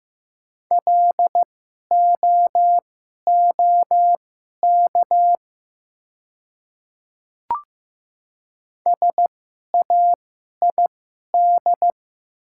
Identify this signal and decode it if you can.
Morse code, heard as a keyed tone, about 15 wpm, spaced as for 11 wpm, 690 Hz. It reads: LOOK SAID
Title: 2x Character Spacing - Top 100 Words - 15wpm